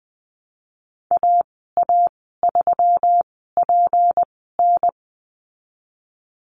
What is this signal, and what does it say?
Morse code, heard as a keyed tone, about 20 wpm, spaced as for 15 wpm, 700 Hz. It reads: AA3PN
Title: 179 Lesson 29g - Slash - Call Signs 20wpm